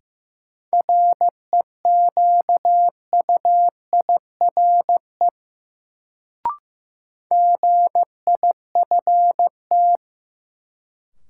Morse code, heard as a keyed tone, about 15 wpm, 700 Hz. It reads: REQUIRE GIFT
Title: Sentences from Top 900 Words - New Word Drill - 15wpm